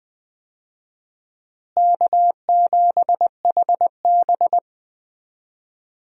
Morse code, heard as a keyed tone, about 20 wpm, 700 Hz. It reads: K7HB
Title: USA Call Signs - 4 Characters - Repeated 3x - 20wpm